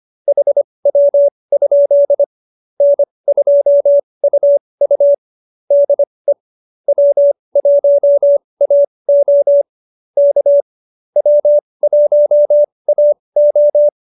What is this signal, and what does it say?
Morse code, heard as a keyed tone, about 25 wpm, spaced as for 20 wpm, 590 Hz. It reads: HW? N2UU DE W1AO K W1AO